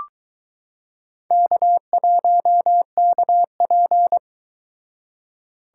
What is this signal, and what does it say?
Morse code, heard as a keyed tone, about 22 wpm, 700 Hz. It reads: K1KP